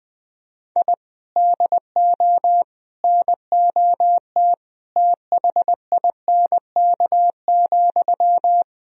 Morse code, heard as a keyed tone, about 21 wpm, 700 Hz. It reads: I DO NOT THINK,
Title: Sentences from Top 1000 Words - Review Part 3 - 20wpm